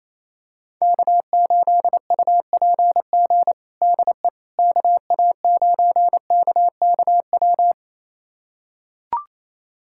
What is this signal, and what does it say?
Morse code, heard as a keyed tone, about 28 wpm, 700 Hz. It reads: K8UPG DE KA9KKW
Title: USA Callsign DE Callsign 28wpm